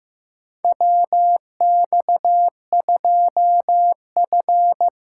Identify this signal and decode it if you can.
Morse code, heard as a keyed tone, about 15 wpm, 700 Hz. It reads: WX2F
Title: USA Call Signs - 2x Character Spacing - 15wpm